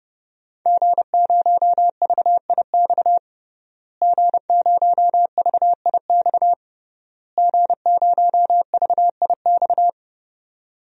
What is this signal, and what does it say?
Morse code, heard as a keyed tone, about 30 wpm, 700 Hz. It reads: G0VIX G0VIX G0VIX
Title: DXCC-100 Call Signs - 5 Characters - Repeated 3x - 30wpm